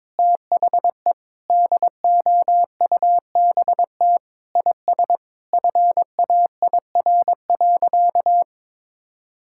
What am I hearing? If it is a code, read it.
Morse code, heard as a keyed tone, about 22 wpm, 700 Hz. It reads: THE DOUBT IS FAIR.
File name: practice / 22wpm / Sentences with 8-Word Phrases - 22wpm